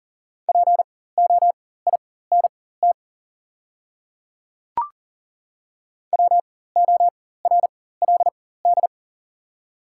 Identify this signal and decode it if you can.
Morse code, heard as a keyed tone, about 40 wpm, spaced as for 19 wpm, 700 Hz. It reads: POINT WORLD
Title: 4x Character Spacing - Top 300 Words - 40wpm